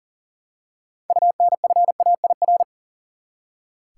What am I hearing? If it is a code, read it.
Morse code, heard as a keyed tone, about 40 wpm, 700 Hz. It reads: UNFAIR
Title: Six Letter Words - 40wpm